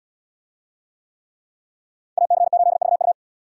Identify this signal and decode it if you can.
Morse code, heard as a keyed tone, about 72 wpm, 700 Hz. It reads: AC8VU